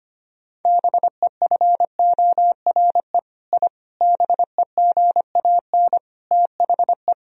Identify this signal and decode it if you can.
Morse code, heard as a keyed tone, about 25 wpm, 690 Hz. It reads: BEFORE I BEGAN THE
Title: Sentences from Top 1000 Words - Review Part 4 - 25wpm